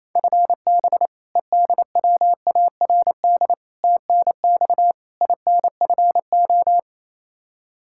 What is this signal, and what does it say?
Morse code, heard as a keyed tone, about 27 wpm, 700 Hz. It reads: FB EDWARD TNX INFO
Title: Practice QSO - Exchange 3 - More Info - 28wpm